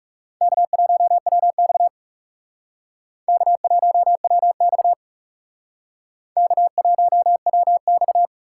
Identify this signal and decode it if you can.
Morse code, heard as a keyed tone, about 38 wpm, 700 Hz. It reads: K1WX K1WX K1WX